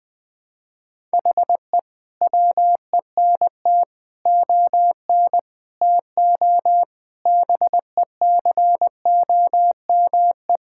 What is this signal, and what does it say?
Morse code, heard as a keyed tone, about 20 wpm, 700 Hz. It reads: HE WENT ON TO BECOME
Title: Sentences with 7-Word Phrases - Spoken Beginning - 20wpm